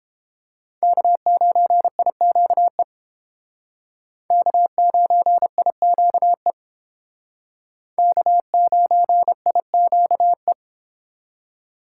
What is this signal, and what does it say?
Morse code, heard as a keyed tone, about 32 wpm, 700 Hz. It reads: K9IQE K9IQE K9IQE